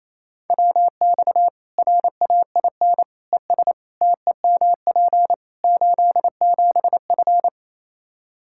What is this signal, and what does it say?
Morse code, heard as a keyed tone, about 28 wpm, 700 Hz. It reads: WX RAIN ES TEMP 87F
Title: Practice QSO - Exchange 2 - Info - 28wpm